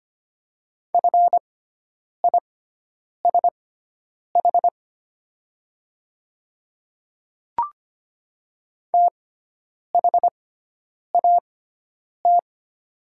Morse code, heard as a keyed tone, about 25 wpm, spaced as for 9 wpm, 700 Hz. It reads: FISH THAT
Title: 6x Character Spacing - Top 500 Words - 25wpm